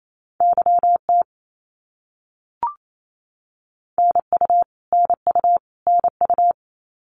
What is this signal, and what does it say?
Morse code, heard as a keyed tone, about 28 wpm, 700 Hz. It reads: YT NU NU NU
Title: Canadian Provinces and Territories - Repeated 3x - 28wpm